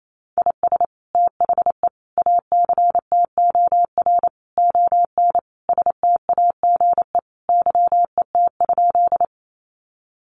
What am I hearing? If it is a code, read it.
Morse code, heard as a keyed tone, about 28 wpm, 700 Hz. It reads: IS THE ACTOR ON STAGE YET?
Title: Sentences from Top 1500 Words 28wpm